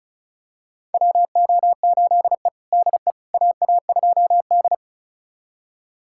Morse code, heard as a keyed tone, about 36 wpm, 700 Hz. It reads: WO8E DE AA2D
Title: USA Callsign DE Callsign - 4 Characters - 35wpm